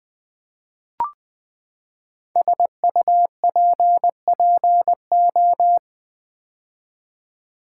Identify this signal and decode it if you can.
Morse code, heard as a keyed tone, about 20 wpm, 700 Hz. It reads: SUPPO
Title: Build A Word - 5 Letters At A Time - 10-Letter Words - 20wpm